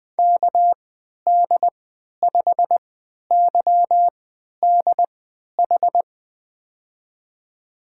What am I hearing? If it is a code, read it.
Morse code, heard as a keyed tone, about 20 wpm, spaced as for 11 wpm, 700 Hz. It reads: KD5YDH
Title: USA Call Signs - 6 Characters - 3x Character Spacing - 20wpm